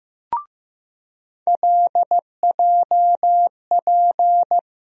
Morse code, heard as a keyed tone, about 16 wpm, 700 Hz. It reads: LJP
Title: USA Call Sign Postfixes - 3 Letters 15wpm